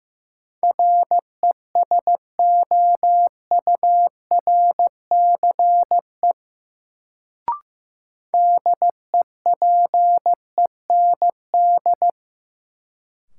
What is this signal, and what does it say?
Morse code, heard as a keyed tone, about 15 wpm, 700 Hz. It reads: RESOURCE DEPEND
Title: Sentences from Top 1300 Words - New Word Drill - 15wpm